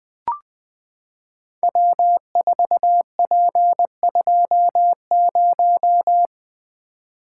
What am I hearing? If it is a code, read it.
Morse code, heard as a keyed tone, about 20 wpm, 700 Hz. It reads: W4P20
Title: Code Groups - 5 Letter:Numbers - 20wpm